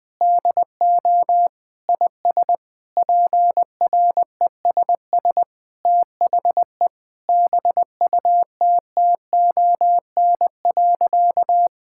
Morse code, heard as a keyed tone, about 21 wpm, 700 Hz. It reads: DO IS PRESS THE BUTTON.
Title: Sentences with 6-Word Phrases - Spoken Beginning - 20wpm